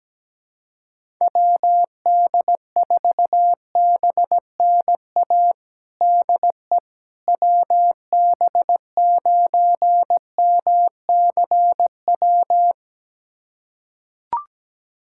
Morse code, heard as a keyed tone, about 17 wpm, 700 Hz. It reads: WD4BNA DE WB9MCW
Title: USA Callsign DE Callsign - 6 Characters - 17wpm